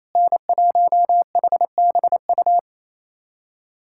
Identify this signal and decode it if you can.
Morse code, heard as a keyed tone, about 27 wpm, 700 Hz. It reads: N1HBU
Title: USA Callsign DE Callsign - 5 Characters - 28wpm